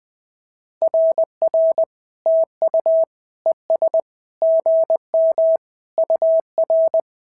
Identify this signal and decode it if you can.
Morse code, heard as a keyed tone, about 20 wpm, 650 Hz. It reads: RR TU ES GM UR
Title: SOTA QSO Practice 20wpm